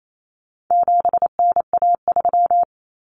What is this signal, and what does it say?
Morse code, heard as a keyed tone, about 28 wpm, 700 Hz. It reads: 7NA3
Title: Common Cut Numbers - 4-Digit 28wpm